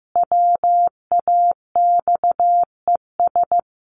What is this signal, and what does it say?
Morse code, heard as a keyed tone, about 15 wpm, 700 Hz. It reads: WAXES
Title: A Sound Of Thunder 15wpm